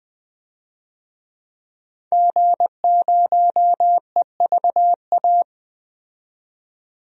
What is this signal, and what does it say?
Morse code, heard as a keyed tone, about 20 wpm, 700 Hz. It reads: G0EVA
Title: DXCC-100 Call Signs - 5 Characters - 20wpm